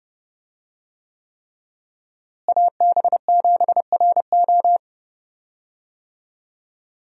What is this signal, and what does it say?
Morse code, heard as a keyed tone, about 30 wpm, 700 Hz. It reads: AB7RO
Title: USA Call Signs - 5 Characters - Repeated 3x - 30wpm